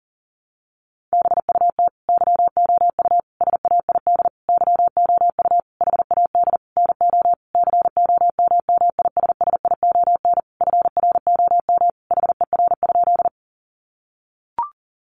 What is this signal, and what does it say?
Morse code, heard as a keyed tone, about 40 wpm, 700 Hz. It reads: BUT YOU SAID YOU HAD NO COMMISSION FROM HER?
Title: Sentences from Top 1400 Words 40wpm